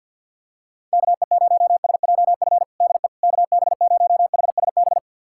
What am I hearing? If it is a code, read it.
Morse code, heard as a keyed tone, about 47 wpm, 700 Hz. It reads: KE0SJF DE KB0HSB
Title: USA Callsign DE Callsign - 6 Characters - 50wpm